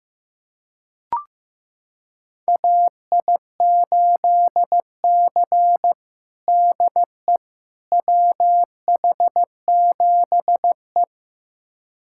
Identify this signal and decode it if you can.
Morse code, heard as a keyed tone, about 15 wpm, 700 Hz. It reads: AI8C DE WH7E
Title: USA Callsign DE Callsign - 4 Characters - 15wpm